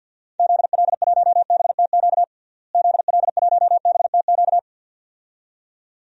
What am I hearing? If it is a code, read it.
Morse code, heard as a keyed tone, about 53 wpm, 700 Hz. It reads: ZL1BTQ ZL1BTQ
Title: DXCC-100 Call Signs - 6 Characters - Repeated 2x - 50wpm